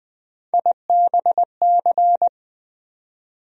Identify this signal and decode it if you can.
Morse code, heard as a keyed tone, about 20 wpm, 700 Hz. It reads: IBC
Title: USA Call Sign Postfixes - 3 Letters 20wpm